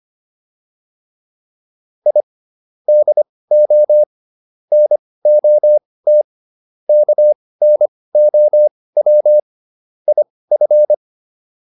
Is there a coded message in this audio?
Yes: Morse code from 2.1 to 10.9 s, heard as a keyed tone, about 25 wpm, spaced as for 18 wpm, 600 Hz. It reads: I DO NOT KNOW IF